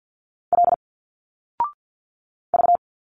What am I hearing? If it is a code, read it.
Morse code, heard as a keyed tone, about 50 wpm, 700 Hz. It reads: L V
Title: Single Letters 50wpm